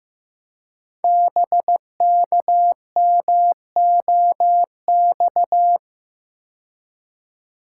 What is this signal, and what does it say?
Morse code, heard as a keyed tone, about 15 wpm, 700 Hz. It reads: BKMOX
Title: Code Groups - 5 Letters - 15wpm